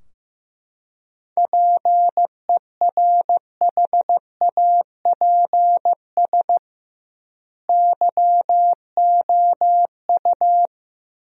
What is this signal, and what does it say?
Morse code, heard as a keyed tone, about 16 wpm, 700 Hz. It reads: PERHAPS YOU
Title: 2x Word-Spacing - Sentences from Top 500 Words - 15wpm